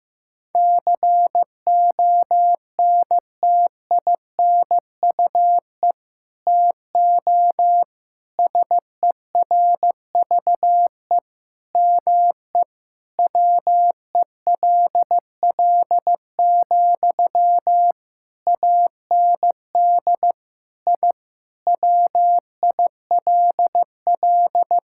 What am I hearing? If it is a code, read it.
Morse code, heard as a keyed tone, about 16 wpm, 700 Hz. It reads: CONTINUE TO SERVE ME WELL, AND I WILL
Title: Sentences from Top 1000 Words - Review Part 6 - 15wpm